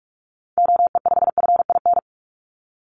Morse code, heard as a keyed tone, about 46 wpm, 700 Hz. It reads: OE5FIN
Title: Call Signs - 45wpm